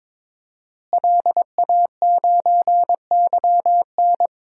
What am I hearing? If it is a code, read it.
Morse code, heard as a keyed tone, about 22 wpm, 700 Hz. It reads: LA9YN
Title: DXCC-100 Call Signs - 5 Characters - Repeated 2x - 22wpm